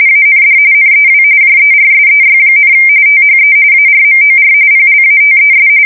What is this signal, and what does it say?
RTTY, about 45.45 baud. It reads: AND NEW JERSEY, THE ARMOR PLATES
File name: fsk170 / USS-Missouri